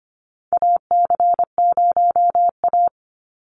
Morse code, heard as a keyed tone, about 25 wpm, 700 Hz. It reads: AC0A
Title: NA SOTA Chaser Callsigns 25wpm